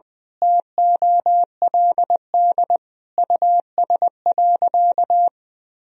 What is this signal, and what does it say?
Morse code, heard as a keyed tone, about 21 wpm, 700 Hz. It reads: TOLD US.
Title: Sentences from Top 1000 Words - Review Part 3 - 20wpm